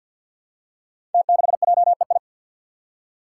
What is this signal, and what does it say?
Morse code, heard as a keyed tone, about 46 wpm, 700 Hz. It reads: T6JEI
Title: Five Challenging Letter-Number Combinations 50wpm